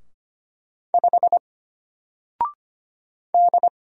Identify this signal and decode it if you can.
Morse code, heard as a keyed tone, about 25 wpm, 700 Hz. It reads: 5 D
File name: Single-Letter-Number-25wpm